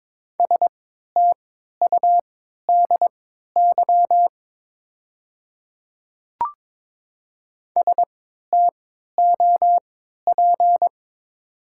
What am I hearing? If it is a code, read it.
Morse code, heard as a keyed tone, about 22 wpm, spaced as for 12 wpm, 700 Hz. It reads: STUDY STOP
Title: 3x Character Spacing - Top 300 Words - 22wpm